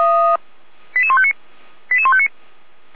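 Five-tone selective call (ZVEI), heard as quick runs of five tones, tones of about 70 ms. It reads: 88159 88159
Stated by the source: Received on an SDR.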